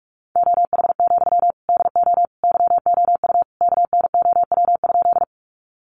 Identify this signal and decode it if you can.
Morse code, heard as a keyed tone, about 44 wpm, 700 Hz. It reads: OH, DO YOU KNOW?
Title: Sentences from Top 500 Words 45wpm